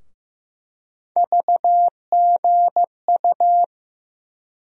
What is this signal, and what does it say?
Morse code, heard as a keyed tone, about 15 wpm, 700 Hz. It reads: VGU
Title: USA Call Sign Postfixes - 3 Letters 15wpm